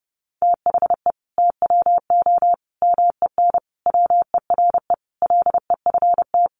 Morse code, heard as a keyed tone, about 30 wpm, 700 Hz. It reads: THE TWO MEN WERE LEFT
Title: Sentences from Top 1000 Words 30wpm